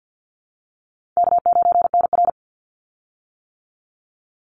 Morse code, heard as a keyed tone, about 50 wpm, 700 Hz. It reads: K9NR
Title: Easy Call Signs - 50wpm